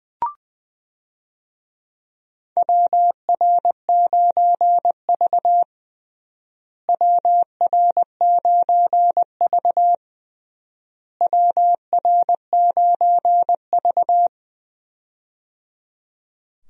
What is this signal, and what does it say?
Morse code, heard as a keyed tone, about 20 wpm, 700 Hz. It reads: WR9V WR9V WR9V
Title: USA Call Signs - 4 Characters - Repeated 3x - 20wpm